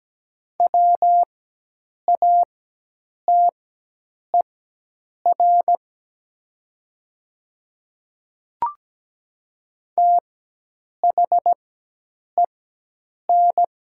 Morse code, heard as a keyed tone, about 17 wpm, spaced as for 8 wpm, 700 Hz. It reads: WATER THEN
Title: 4x Character Spacing - Top 100 Words - 17wpm